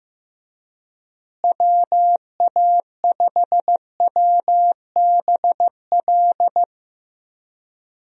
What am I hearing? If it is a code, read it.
Morse code, heard as a keyed tone, about 15 wpm, 700 Hz. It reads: WA5WBL
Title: USA Call Signs - 6 Characters - Repeated 3x - 15wpm